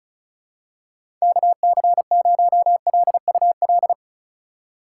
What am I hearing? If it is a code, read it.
Morse code, heard as a keyed tone, about 34 wpm, 700 Hz. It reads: KC0LUL
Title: USA Call Signs - 6 Characters - 4x Character Spacing - 35wpm